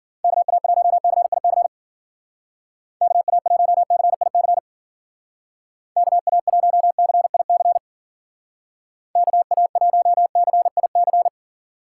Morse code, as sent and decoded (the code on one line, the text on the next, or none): -.- .- .---- -.-. .. -.-. / -.- .- .---- -.-. .. -.-. / -.- .- .---- -.-. .. -.-. / -.- .- .---- -.-. .. -.-.
KA1CIC KA1CIC KA1CIC KA1CIC